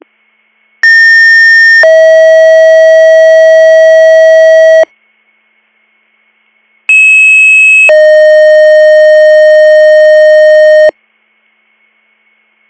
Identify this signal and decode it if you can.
Two-tone page, heard as a one-second tone, then a three-second tone, about 1800 then 640 Hz, about 2600 then 620 Hz. It is heard from 0.8 to 10.9 s.